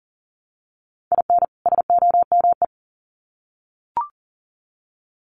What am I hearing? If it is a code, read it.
Morse code, heard as a keyed tone, about 40 wpm, 700 Hz. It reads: IN SOME
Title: Sets of 2 Words 40wpm